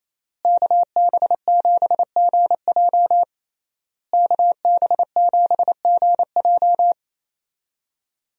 Morse code, heard as a keyed tone, about 27 wpm, 700 Hz. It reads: KB7GJ KB7GJ
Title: USA Call Signs - 5 Characters - Repeated 3x - 28wpm